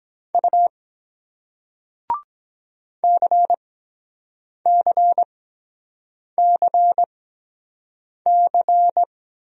Morse code, heard as a keyed tone, about 23 wpm, spaced as for 7 wpm, 700 Hz. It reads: U CCCC